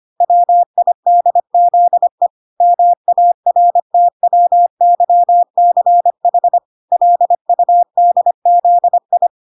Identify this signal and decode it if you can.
Morse code, heard as a keyed tone, about 25 wpm, 700 Hz. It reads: WIDZE MARTWYCH LUDZI